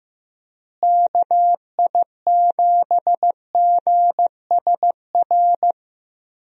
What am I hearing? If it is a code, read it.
Morse code, heard as a keyed tone, about 15 wpm, 700 Hz. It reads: KI7GSR
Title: USA Call Signs - 6 Characters - Repeated 3x - 15wpm